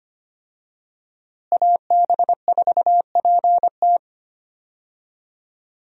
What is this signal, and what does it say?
Morse code, heard as a keyed tone, about 25 wpm, 700 Hz. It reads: AB4PT
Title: USA Call Signs - 5 Characters - Repeated 3x - 25wpm